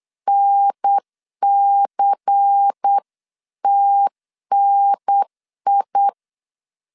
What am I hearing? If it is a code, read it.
Morse code, heard as a keyed tone, about 8 wpm, 800 Hz. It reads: NCTNI
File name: 8wpm